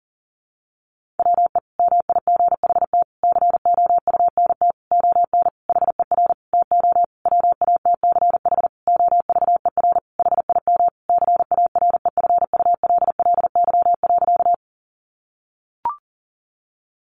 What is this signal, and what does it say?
Morse code, heard as a keyed tone, about 37 wpm, 700 Hz. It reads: WE MIGHT COUNT ON HER TO WATCH OVER HIM CAREFULLY.
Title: Sentences from Top 1500 Words 40wpm